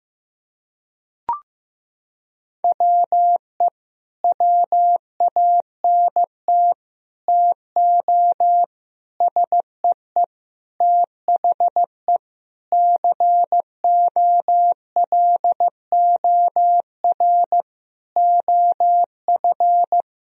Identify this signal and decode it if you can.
Morse code, heard as a keyed tone, about 16 wpm, 700 Hz. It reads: WE WANT TO SEE THE COLOR OF
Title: Sentences from Top 1000 Words - Review Part 2 - 15wpm